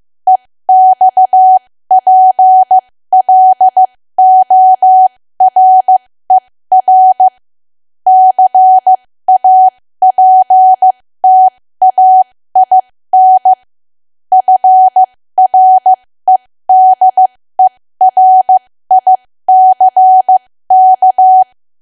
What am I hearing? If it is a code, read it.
Morse code, heard as a keyed tone, about 15 wpm, spaced as for 13 wpm, 750 Hz. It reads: EXPLORER CAPTAIN FREDERICK
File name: cwa / W1AW-Level2 / 13WPM